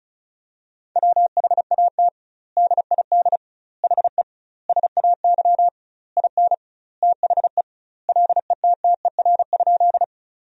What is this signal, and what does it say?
Morse code, heard as a keyed tone, about 36 wpm, 700 Hz. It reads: WHAT DID HE SAY IN THE LETTER?
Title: 2x Word-Spacing - Sentences from Top 300 Words - 35wpm